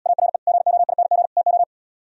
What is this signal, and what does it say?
Morse code, heard as a keyed tone, about 77 wpm, 700 Hz. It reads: ARE GOING TO